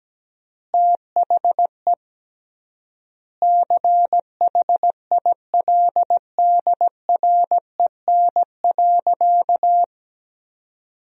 Morse code, heard as a keyed tone, about 17 wpm, 700 Hz. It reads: THE CHILDREN.
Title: 3x Word-Spacing - Sentences from Top 300 Words - 17wpm